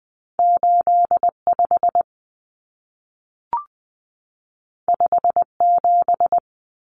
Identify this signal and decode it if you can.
Morse code, heard as a keyed tone, about 20 wpm, 700 Hz. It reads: 85 57
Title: Common Cut Numbers - 2-Digit 20wpm